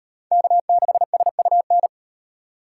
Morse code, heard as a keyed tone, about 36 wpm, 700 Hz. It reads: K6SUN